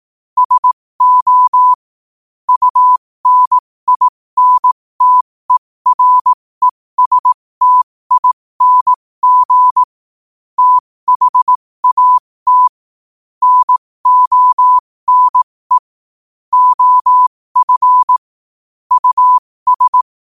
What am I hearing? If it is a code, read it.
Morse code, heard as a keyed tone, about 18 wpm, spaced as for 15 wpm, 1000 Hz. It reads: SO UNINTERESTING THAT NONE OF US
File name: qotdfiles / 15wpm